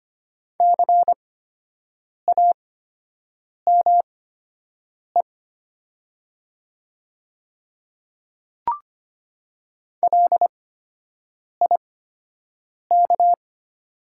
Morse code, heard as a keyed tone, about 25 wpm, spaced as for 7 wpm, 700 Hz. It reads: CAME LIK
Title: 8x Character Spacing - Top 300 Words - 25wpm